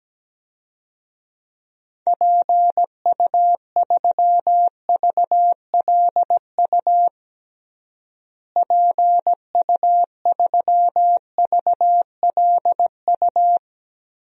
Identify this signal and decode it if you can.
Morse code, heard as a keyed tone, about 17 wpm, 700 Hz. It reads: PU3VLU PU3VLU
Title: DXCC-100 Call Signs - 6 Characters - Repeated 2x - 17wpm